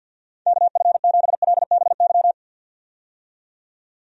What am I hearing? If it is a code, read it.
Morse code, heard as a keyed tone, about 50 wpm, 700 Hz. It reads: KF7LBY
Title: USA Call Signs - 6 Characters - 50wpm